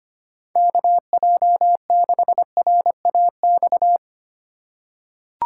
Morse code, heard as a keyed tone, about 25 wpm, 700 Hz. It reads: KJ6RAX
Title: USA Call Signs - 6 Characters - 4x Character Spacing - 25wpm